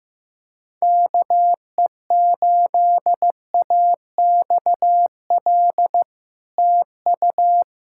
Morse code, heard as a keyed tone, about 15 wpm, 700 Hz. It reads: KE8AXL TU
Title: SOTA QSO Practice 15wpm